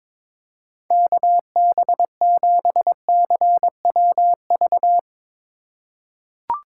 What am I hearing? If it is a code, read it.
Morse code, heard as a keyed tone, about 22 wpm, 700 Hz. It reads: KB7CWV
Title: USA Call Signs - 6 Characters - 2x Character Spacing - 22wpm